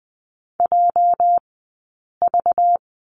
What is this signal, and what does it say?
Morse code, heard as a keyed tone, about 20 wpm, spaced as for 8 wpm, 700 Hz. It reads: JV
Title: 211 Lesson 34d - Number 8 - Two Characters 20wpm